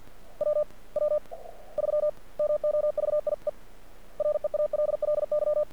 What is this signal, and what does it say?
Morse code, heard as a keyed tone, about 50 wpm, spaced as for 36 wpm, 600 Hz. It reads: WW3 MOVIE REALLY
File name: archived / CRSnet